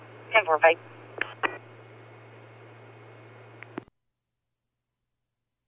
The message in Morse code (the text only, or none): none